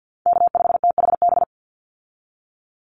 Morse code, heard as a keyed tone, about 50 wpm, 700 Hz. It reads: K5THB
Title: USA Call Signs 50wpm